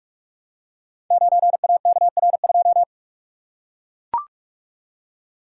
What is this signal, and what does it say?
Morse code, heard as a keyed tone, about 44 wpm, 700 Hz. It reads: 9AKR2
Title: Code Groups - 5 Letter:Numbers - 45wpm